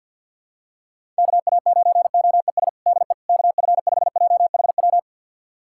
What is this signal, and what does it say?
Morse code, heard as a keyed tone, about 52 wpm, 700 Hz. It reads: KA9OES DE KF5JHW
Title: USA Callsign DE Callsign 50wpm